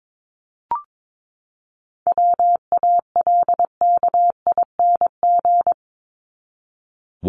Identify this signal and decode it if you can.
Morse code, heard as a keyed tone, about 22 wpm, 700 Hz. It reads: WALKING
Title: ING Suffix - Top 100 Words with Base Words 22wpm